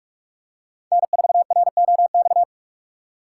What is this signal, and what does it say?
Morse code, heard as a keyed tone, about 44 wpm, 700 Hz. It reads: N4ROX